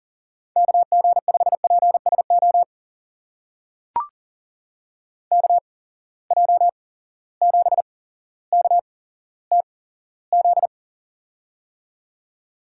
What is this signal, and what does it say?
Morse code, heard as a keyed tone, about 39 wpm, spaced as for 17 wpm, 700 Hz. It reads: KG5PSO KJ7KTZ
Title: USA Call Signs - 8x Character Spacing - 40wpm